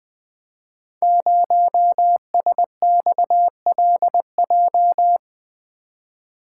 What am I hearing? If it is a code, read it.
Morse code, heard as a keyed tone, about 20 wpm, 700 Hz. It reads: 0SXLJ
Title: Code Groups - 5 Letter:Numbers - 20wpm